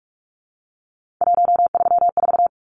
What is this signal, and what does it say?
Morse code, heard as a keyed tone, about 44 wpm, 700 Hz. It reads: 134
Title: Numbers - 3-Digit 45wpm